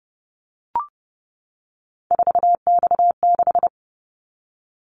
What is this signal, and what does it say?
Morse code, heard as a keyed tone, about 30 wpm, 700 Hz. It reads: 4X6
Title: Callsign Prefixes for Top 50 DXCC - 30wpm